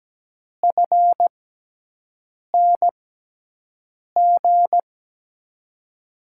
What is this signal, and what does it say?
Morse code, heard as a keyed tone, about 17 wpm, spaced as for 6 wpm, 700 Hz. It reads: FNG
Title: USA Call Signs - 6 Characters - 6x Character Spacing - 17wpm